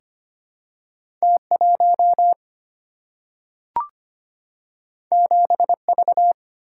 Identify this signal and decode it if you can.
Morse code, heard as a keyed tone, about 25 wpm, 700 Hz. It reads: T1 7V
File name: Two Letter-Number Combinations - 25wpm